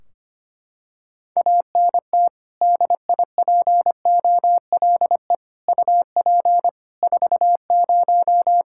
Morse code, heard as a keyed tone, about 25 wpm, 700 Hz. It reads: ANT DIPOLE UP 40
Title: Practice QSO - Exchange 2 - Info - 25wpm